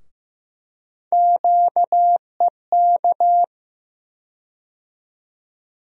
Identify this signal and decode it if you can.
Morse code, heard as a keyed tone, about 15 wpm, 700 Hz. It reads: QEK
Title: USA Call Sign Postfixes - 3 Letters 15wpm